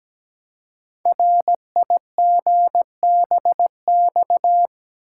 Morse code, heard as a keyed tone, about 17 wpm, 700 Hz. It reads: RIGBX
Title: Code Groups - 5 Letters - 17wpm